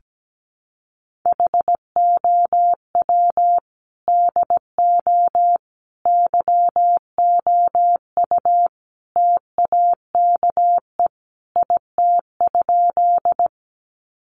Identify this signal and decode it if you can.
Morse code, heard as a keyed tone, about 17 wpm, 700 Hz. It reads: HOW DO YOU TAKE IT?
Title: Sentences from Top 200 Words 17wpm